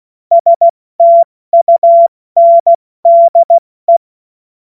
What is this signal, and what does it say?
Morse code, heard as a keyed tone, about 16 wpm, 680 Hz. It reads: STUNDE
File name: Morsekurs / Witze / Witz 01. August 2023 70 F80